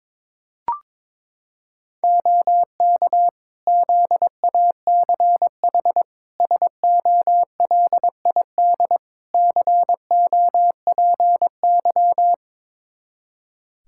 Morse code, heard as a keyed tone, about 22 wpm, 700 Hz. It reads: OK ZACH SOLID COPY
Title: K1USN SST QSO Practice 22wpm